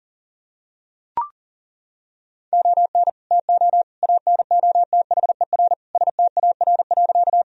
Morse code, heard as a keyed tone, about 40 wpm, 700 Hz. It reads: ON TO ANOTHER STAR.
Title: Sentences from Top 1000 Words - Review Part 2 - 40wpm